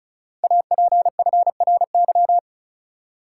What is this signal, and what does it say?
Morse code, heard as a keyed tone, about 35 wpm, 700 Hz. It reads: APFRY
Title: Code Groups - 5 Letters - 35wpm